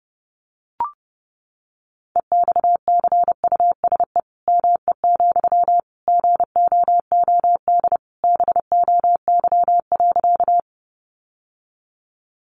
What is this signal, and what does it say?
Morse code, heard as a keyed tone, about 30 wpm, 700 Hz. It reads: EXCUSE ME, GOOD BOY.
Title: Sentences from Top 700 Words 30wpm